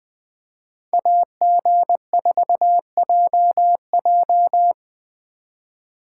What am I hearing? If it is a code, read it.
Morse code, heard as a keyed tone, about 20 wpm, 700 Hz. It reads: AG4JJ
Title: Speed Racing - USA 2x2 Call Signs 17wpm